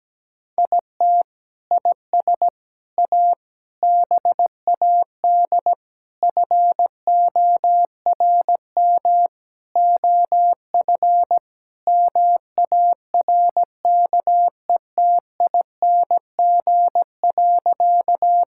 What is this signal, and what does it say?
Morse code, heard as a keyed tone, about 17 wpm, 700 Hz. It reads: IT IS A BAD FORM OF MARKETING.
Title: Sentences from Top 1000 Words - Review Part 5 - 17wpm